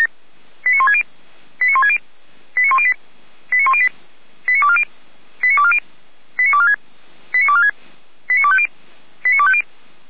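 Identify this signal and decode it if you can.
Five-tone selective call (ZVEI), heard as quick runs of five tones, tones of about 70 ms. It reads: 89160 89160 89108 89108 89240 89240 89246 89246 89250 89250
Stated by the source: Five-tone ZVEI selective call received on SDR.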